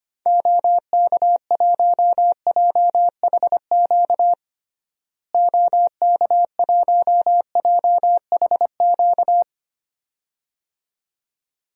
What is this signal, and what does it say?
Morse code, heard as a keyed tone, about 25 wpm, 700 Hz. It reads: OK1JHQ OK1JHQ
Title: DXCC-100 Call Signs - 6 Characters - Repeated 2x - 25wpm